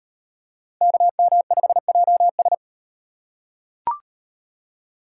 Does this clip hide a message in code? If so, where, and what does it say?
Yes: Morse code from 0.8 to 2.5 s, heard as a keyed tone, about 38 wpm, 700 Hz. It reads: KM5JS